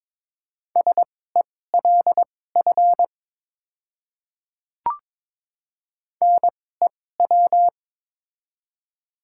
Morse code, heard as a keyed tone, about 22 wpm, spaced as for 16 wpm, 700 Hz. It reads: SELF NEW
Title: 2x Character Spacing - Top 300 Words - 22wpm